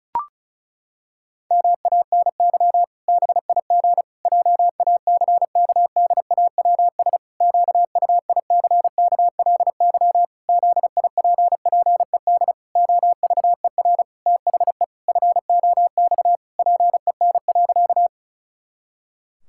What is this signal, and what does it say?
Morse code, heard as a keyed tone, about 34 wpm, 700 Hz. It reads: MANY BIG JACKDAWS QUICKLY ZIPPED OVER THE FOX PEN.
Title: Pangram Challenge v2 - 35wpm